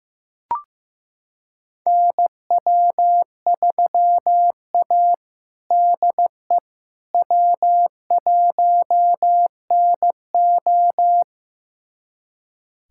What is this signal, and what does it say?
Morse code, heard as a keyed tone, about 15 wpm, 700 Hz. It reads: NW3A DE W1NO
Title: USA Callsign DE Callsign - 4 Characters - 15wpm